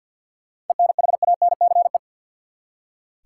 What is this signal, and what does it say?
Morse code, heard as a keyed tone, about 46 wpm, 700 Hz. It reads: ENHANCE
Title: Seven Letter Words - 50wpm